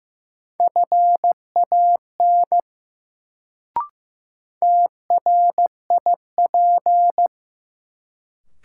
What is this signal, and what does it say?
Morse code, heard as a keyed tone, about 15 wpm, 700 Hz. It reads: FAN TRIP
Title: Sentences from Top 1000 Words - Review Part 4 - New Word Drill - 15wpm